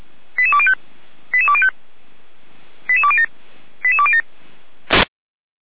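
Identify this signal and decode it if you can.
Five-tone selective call (ZVEI), heard as quick runs of five tones, tones of about 70 ms. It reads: 88205 88205 88207 88207
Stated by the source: Five-tone ZVEI selective call 88179 received by SDR.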